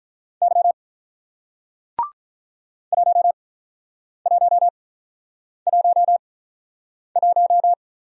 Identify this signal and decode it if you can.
Morse code, heard as a keyed tone, about 48 wpm, spaced as for 9 wpm, 700 Hz. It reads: Y 1111